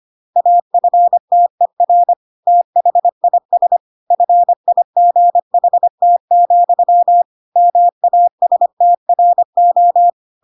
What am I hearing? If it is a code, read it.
Morse code, heard as a keyed tone, about 25 wpm, 700 Hz. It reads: AFTER THIS FIGHT, MASTRO
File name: Pinocchio / Pinocchio ch2 25 25 ik1hge